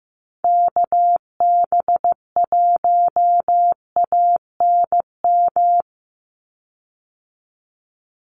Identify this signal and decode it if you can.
Morse code, heard as a keyed tone, about 15 wpm, 700 Hz. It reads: KB1ANM
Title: USA Call Signs 15wpm